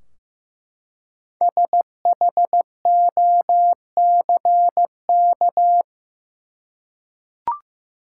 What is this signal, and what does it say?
Morse code, heard as a keyed tone, about 15 wpm, 700 Hz. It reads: SHOCK
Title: Five Letter Words - 15wpm